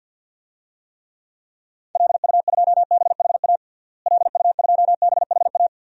Morse code, heard as a keyed tone, about 50 wpm, 700 Hz. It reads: LU2BHA LU2BHA
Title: DXCC-100 Call Signs - 6 Characters - Repeated 2x - 50wpm